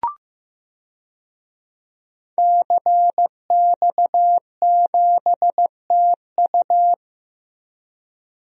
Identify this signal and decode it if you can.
Morse code, heard as a keyed tone, about 15 wpm, 700 Hz. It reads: CX7TU
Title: DXCC-100 Call Signs - 5 Characters - Repeated 3x - 15wpm